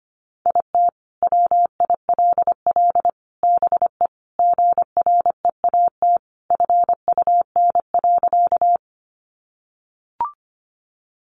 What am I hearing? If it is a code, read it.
Morse code, heard as a keyed tone, about 25 wpm, 700 Hz. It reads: IT WILL BE GREAT FUN.
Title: Sentences from Top 700 Words 25wpm